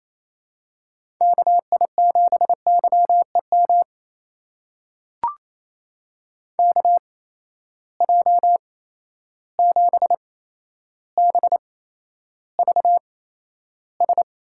Morse code, heard as a keyed tone, about 28 wpm, spaced as for 12 wpm, 700 Hz. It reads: KI7YEM KJ7BVS
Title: USA Call Signs - 6 Characters - 8x Character Spacing - 28wpm